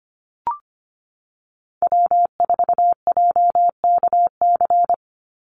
Morse code, heard as a keyed tone, about 25 wpm, 700 Hz. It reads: W4JKC
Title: NA SOTA Chaser Callsigns 25wpm